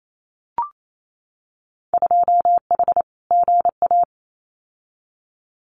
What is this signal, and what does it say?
Morse code, heard as a keyed tone, about 28 wpm, 700 Hz. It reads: 2H GA
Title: Winter Field Day Exchanges Encore 28wpm